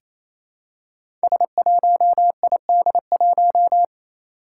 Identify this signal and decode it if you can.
Morse code, heard as a keyed tone, about 27 wpm, 700 Hz. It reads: S1ID1
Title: Five Challenging Letter-Number Combinations 28wpm